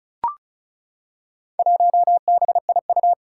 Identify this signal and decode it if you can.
Morse code, heard as a keyed tone, about 36 wpm, 700 Hz. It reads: 1BIU
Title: Four Challenging Letter-Number Combinations 35wpm